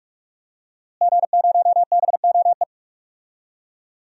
Morse code, heard as a keyed tone, about 42 wpm, 700 Hz. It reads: G0BOE
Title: DXCC-100 Call Signs - 5 Characters - Repeated 2x - 45wpm